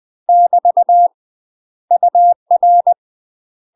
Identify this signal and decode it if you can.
Morse code, heard as a keyed tone, about 20 wpm, 700 Hz. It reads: <BT> UR
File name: UK MASTER QSOs TRAINING / 20WPM / UK MASTER QSOs 0071